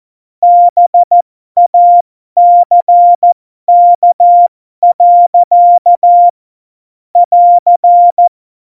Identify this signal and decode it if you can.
Morse code, heard as a keyed tone, about 14 wpm, spaced as for 12 wpm, 700 Hz. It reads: BACK. <AR>
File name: Morsekurs / Witze